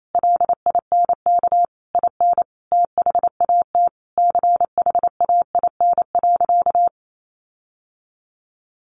Morse code, heard as a keyed tone, about 28 wpm, 700 Hz. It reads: LINK IN THAT CHAIN.
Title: Sentences from Top 1500 Words 28wpm